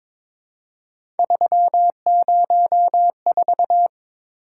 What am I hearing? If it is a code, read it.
Morse code, heard as a keyed tone, about 22 wpm, 700 Hz. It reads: 304
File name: practice / 22wpm / Three Letter-Number Combinations - 22wpm